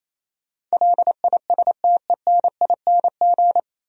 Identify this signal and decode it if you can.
Morse code, heard as a keyed tone, about 37 wpm, 700 Hz. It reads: LISTENING